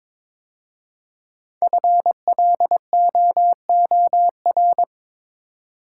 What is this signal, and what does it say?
Morse code, heard as a keyed tone, about 22 wpm, 700 Hz. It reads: FLOOR
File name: 5 Letter Words - First 3 Letters Given - 22wpm